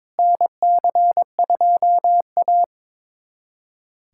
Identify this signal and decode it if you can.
Morse code, heard as a keyed tone, about 22 wpm, 700 Hz. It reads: NC2A
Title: USA Callsign DE Callsign - 4 Characters - 22wpm